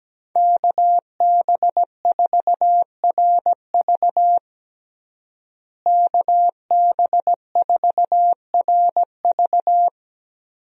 Morse code, heard as a keyed tone, about 17 wpm, 700 Hz. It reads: KB4RV KB4RV
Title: USA Call Signs - 5 Characters - Repeated 3x - 17wpm